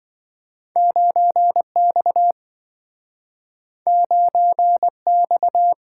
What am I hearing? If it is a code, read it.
Morse code, heard as a keyed tone, about 27 wpm, 700 Hz. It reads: 9X 9X